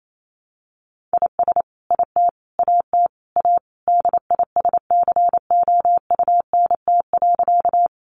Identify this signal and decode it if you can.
Morse code, heard as a keyed tone, about 28 wpm, 700 Hz. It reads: IS IT AT A DISCOUNT.
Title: Sentences from Top 1200 Words 28wpm